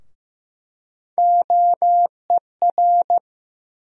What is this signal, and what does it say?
Morse code, heard as a keyed tone, about 15 wpm, 700 Hz. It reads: OER
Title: USA Call Sign Postfixes - 3 Letters 15wpm